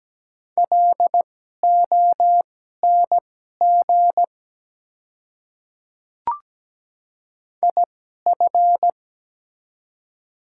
Morse code, heard as a keyed tone, about 17 wpm, spaced as for 12 wpm, 700 Hz. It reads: LONG IF
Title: 2x Character Spacing - Top 100 Words - 17wpm